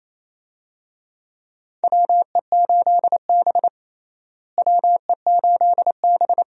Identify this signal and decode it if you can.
Morse code, heard as a keyed tone, about 28 wpm, 700 Hz. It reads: WE8B WE8B
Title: USA Call Signs - 4 Characters - Repeated 2x - 28wpm